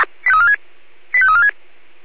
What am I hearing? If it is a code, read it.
Five-tone selective call (ZVEI), heard as quick runs of five tones, tones of about 70 ms. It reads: 85346